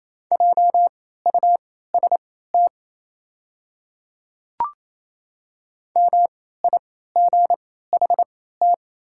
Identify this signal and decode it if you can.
Morse code, heard as a keyed tone, about 28 wpm, spaced as for 16 wpm, 690 Hz. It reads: JUST MIGHT
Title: 3x Character Spacing - Top 500 Words - 28wpm